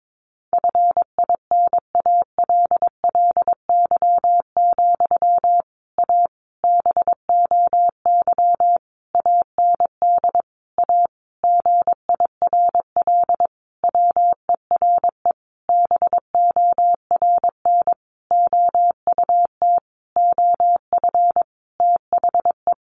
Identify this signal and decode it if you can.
Morse code, heard as a keyed tone, about 22 wpm, 700 Hz. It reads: FINALLY, A BOY AND A GIRL WERE BORN OUT OF THE
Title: Sentences from Top 1200 Words 22wpm